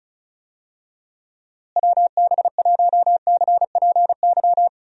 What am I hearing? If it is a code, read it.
Morse code, heard as a keyed tone, about 34 wpm, 700 Hz. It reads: WB1CPY
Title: USA Call Signs - 6 Characters - Repeated 2x - 35wpm